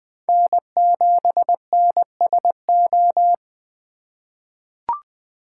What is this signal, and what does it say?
Morse code, heard as a keyed tone, about 20 wpm, 700 Hz. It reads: N7NSO
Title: USA Call Signs - 5 Characters - 4x Character Spacing - 20wpm